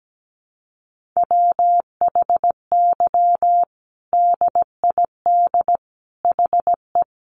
Morse code, heard as a keyed tone, about 17 wpm, 700 Hz. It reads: WHY DID HE
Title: Sentences from Top 1100 Words 17wpm